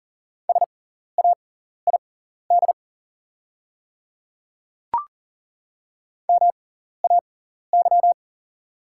Morse code, heard as a keyed tone, about 40 wpm, spaced as for 14 wpm, 700 Hz. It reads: SAID MAY
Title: 6x Character Spacing - Top 200 Words - 40wpm